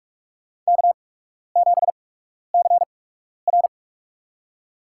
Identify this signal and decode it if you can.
Morse code, heard as a keyed tone, about 46 wpm, spaced as for 12 wpm, 700 Hz. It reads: K7CR
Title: USA Call Signs - 4 Characters - 8x Character Spacing - 45wpm